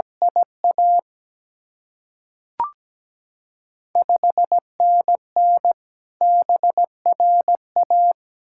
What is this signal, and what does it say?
Morse code, heard as a keyed tone, about 17 wpm, 690 Hz. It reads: IA 5NN BRA
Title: Ontario QSO Party Exchange 17wpm